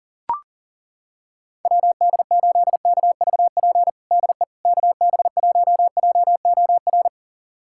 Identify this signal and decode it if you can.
Morse code, heard as a keyed tone, about 40 wpm, 700 Hz. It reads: WD8KVP DE KB1JOR
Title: USA Callsign DE Callsign - 6 Characters - 40wpm